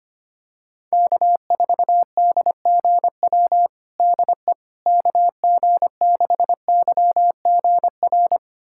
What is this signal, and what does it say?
Morse code, heard as a keyed tone, about 25 wpm, 700 Hz. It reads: K4DGW DE KG6YGR
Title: USA Callsign DE Callsign 25wpm